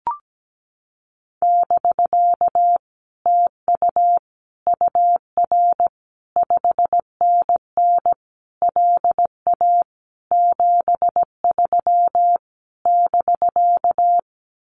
Morse code, heard as a keyed tone, about 17 wpm, 700 Hz. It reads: <BK> TU UR 5NN LA 73 <BK>
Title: Common QSO Phrases Encore 17wpm